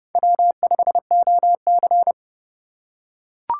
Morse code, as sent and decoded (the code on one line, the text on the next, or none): .-- ..... --- -.-.
W5OC